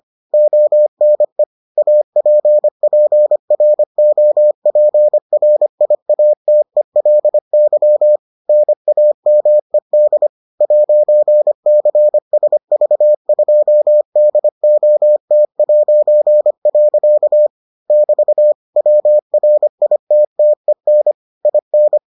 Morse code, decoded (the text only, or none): ONE APPROPRIATELY NAMED 'CSV2DOT'. <BT> WRITTEN IN